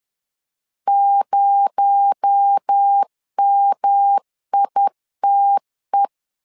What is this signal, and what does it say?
Morse code, heard as a keyed tone, about 10 wpm, 800 Hz. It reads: 0MITE
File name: LEZIONI / 8 YJ45 / 8 YJ 10wpm 2651